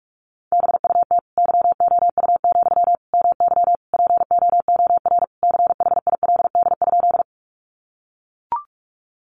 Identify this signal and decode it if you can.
Morse code, heard as a keyed tone, about 44 wpm, 700 Hz. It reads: BUT YOU, MY POOR CHILD?
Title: Sentences from Top 1000 Words 45wpm